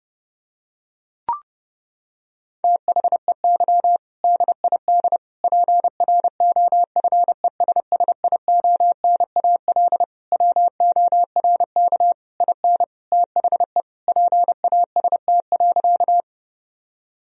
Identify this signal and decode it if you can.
Morse code, heard as a keyed tone, about 30 wpm, 700 Hz. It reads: THEY DID PROFESSIONAL WORK IN THE PAST.
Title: Sentences from Top 1000 Words - Review Part 4 - 30wpm